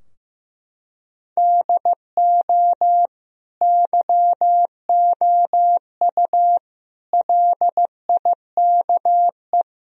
Morse code, heard as a keyed tone, about 15 wpm, 700 Hz. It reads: DO YOU LIKE
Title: Sentences from Top 1000 Words - Review Part 3 - 15wpm